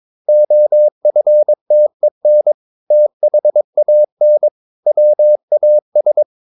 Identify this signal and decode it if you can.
Morse code, heard as a keyed tone, about 22 wpm, 600 Hz. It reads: OFTEN THAN WAS
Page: Romanen Never af Ken Follet på engelsk.